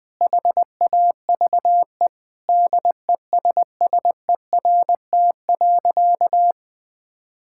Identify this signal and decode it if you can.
Morse code, heard as a keyed tone, about 21 wpm, 700 Hz. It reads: HAVE DESSERT.
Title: Sentences with 5-Word Phrases - Spoken Beginning - 20wpm